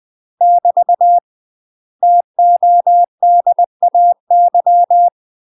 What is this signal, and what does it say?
Morse code, heard as a keyed tone, about 20 wpm, 700 Hz. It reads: <BT> TODAY
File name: UK MASTER QSOs TRAINING / 20WPM / UK MASTER QSOs 0010